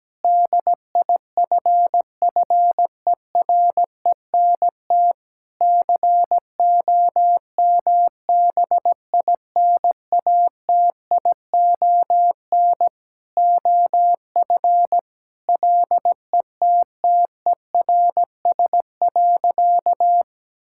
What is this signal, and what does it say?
Morse code, heard as a keyed tone, about 17 wpm, 700 Hz. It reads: DIFFERENT COMBINATION OF LETTERS.
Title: Sentences from Top 1000 Words - Review Part 6 - 17wpm